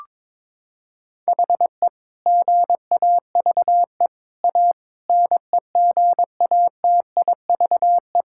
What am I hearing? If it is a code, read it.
Morse code, heard as a keyed tone, about 21 wpm, 700 Hz. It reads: HE GAVE A NEGATIVE
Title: Sentences from Top 1000 Words - Review Part 6 - 22wpm